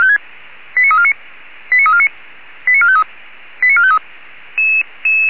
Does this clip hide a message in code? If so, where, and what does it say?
Yes: five-tone selective call (ZVEI) from 0.8 to 4.0 s, heard as quick runs of five tones, tones of about 80 ms. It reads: 89349 89349 89453 89453